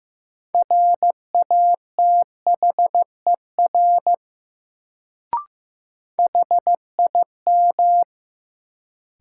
Morse code, heard as a keyed tone, about 15 wpm, 700 Hz. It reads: RATHER HIM
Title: Sets of 2 Words - New Word Drill - 15wpm